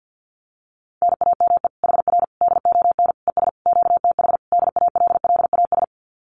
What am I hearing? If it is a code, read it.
Morse code, heard as a keyed tone, about 48 wpm, 700 Hz. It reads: NAME HR DON ES QTH DALLAS
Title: Common QSO Phrases Encore 50wpm